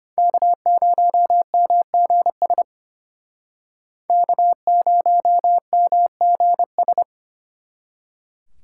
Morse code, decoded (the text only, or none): K0MGS K0MGS